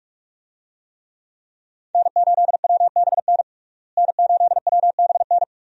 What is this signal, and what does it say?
Morse code, heard as a keyed tone, about 41 wpm, 700 Hz. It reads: N8WBN N8WBN
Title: USA Call Signs - 5 Characters - Repeated 2x - 45wpm